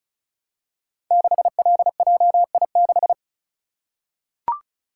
Morse code, heard as a keyed tone, about 36 wpm, 690 Hz. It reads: 6LJI6
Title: Code Groups - 5 Letter:Numbers - 35wpm